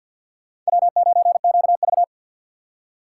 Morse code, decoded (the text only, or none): W9QV